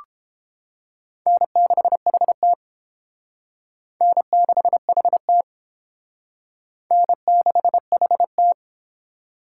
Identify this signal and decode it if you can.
Morse code, heard as a keyed tone, about 29 wpm, 700 Hz. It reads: N6HT N6HT N6HT